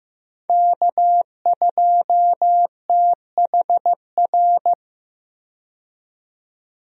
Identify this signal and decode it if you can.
Morse code, heard as a keyed tone, about 15 wpm, 700 Hz. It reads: K2THR
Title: USA Call Signs - 5 Characters - 15wpm